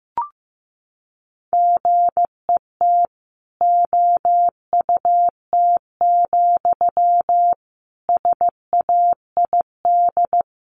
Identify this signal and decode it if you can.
Morse code, heard as a keyed tone, about 15 wpm, 700 Hz. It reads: GET OUT, SAID
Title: A Sound Of Thunder 15wpm